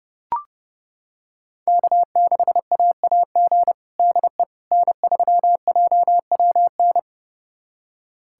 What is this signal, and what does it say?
Morse code, heard as a keyed tone, about 30 wpm, 700 Hz. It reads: K6AAG DE N3JWN
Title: USA Callsign DE Callsign - 5 Characters - 30wpm